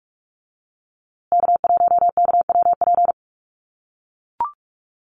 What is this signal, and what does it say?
Morse code, heard as a keyed tone, about 44 wpm, 700 Hz. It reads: K1KWP
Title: USA Call Signs Repeated 3x 45wpm